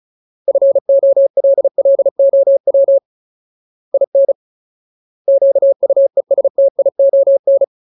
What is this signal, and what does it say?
Morse code, heard as a keyed tone, about 34 wpm, 550 Hz. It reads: FOLLOW IN QUESTION
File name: Ultimate Word List / 35WPM Ultimate